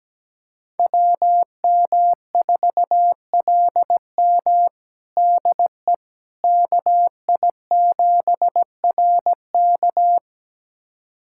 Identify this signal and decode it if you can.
Morse code, heard as a keyed tone, about 17 wpm, 700 Hz. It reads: WM4LM DE KI7RK
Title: USA Callsign DE Callsign - 5 Characters - 17wpm